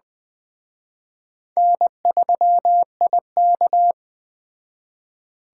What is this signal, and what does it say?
Morse code, heard as a keyed tone, about 20 wpm, 700 Hz. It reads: N3IK
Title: Speed Racing - USA 1x2 Call Signs 17wpm